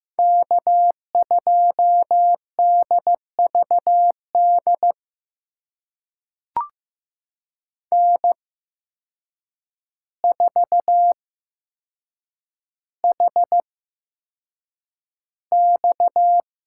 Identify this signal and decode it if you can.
Morse code, heard as a keyed tone, about 15 wpm, spaced as for 7 wpm, 700 Hz. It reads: K2DVD N4HX
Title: USA Call Signs - 5 Characters - 8x Character Spacing - 15wpm